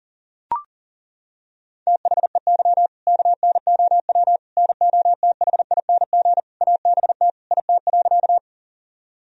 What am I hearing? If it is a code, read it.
Morse code, heard as a keyed tone, about 37 wpm, 700 Hz. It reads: THEY KNOW NOTHING ABT IT.
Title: Sentences from Top 500 Words and Abbreviations 40wpm